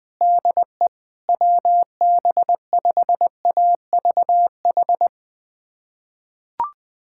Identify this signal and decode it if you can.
Morse code, heard as a keyed tone, about 21 wpm, 700 Hz. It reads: DE WB5AVH
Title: USA Callsign DE Callsign - 6 Characters - 20wpm